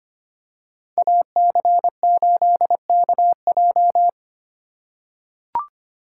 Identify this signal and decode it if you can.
Morse code, heard as a keyed tone, about 25 wpm, 700 Hz. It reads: AC8KJ
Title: USA Call Signs - 5 Characters - 4x Character Spacing - 25wpm